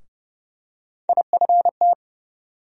Morse code, heard as a keyed tone, about 30 wpm, 700 Hz. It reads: IFT
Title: USA Call Sign Postfixes - 3 Letters 30wpm